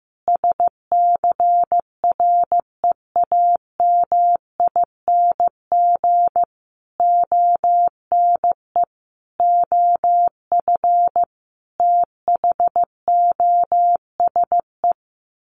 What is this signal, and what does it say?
Morse code, heard as a keyed tone, about 15 wpm, 700 Hz. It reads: SCREAMING ONE OF THOSE
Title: A Sound Of Thunder 15wpm